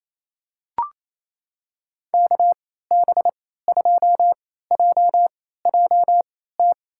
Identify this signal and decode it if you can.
Morse code, heard as a keyed tone, about 28 wpm, spaced as for 16 wpm, 700 Hz. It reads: KB2JJT
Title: USA Call Signs - 6 Characters - 3x Character Spacing - 28wpm